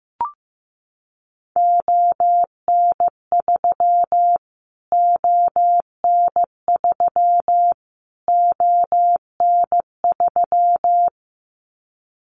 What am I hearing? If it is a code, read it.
Morse code, heard as a keyed tone, about 15 wpm, 700 Hz. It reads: ON3 ON3 ON3
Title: Callsign Prefixes for Top 25 DXCC - Repeated 3x - 15wpm